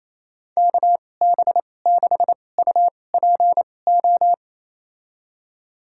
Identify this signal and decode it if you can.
Morse code, heard as a keyed tone, about 28 wpm, spaced as for 20 wpm, 700 Hz. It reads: KB6UPO
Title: USA Call Signs - 6 Characters - 2x Character Spacing - 28wpm